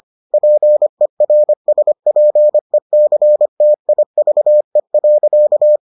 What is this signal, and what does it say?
Morse code, heard as a keyed tone, about 25 wpm, 600 Hz. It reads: PERSPECTIVE.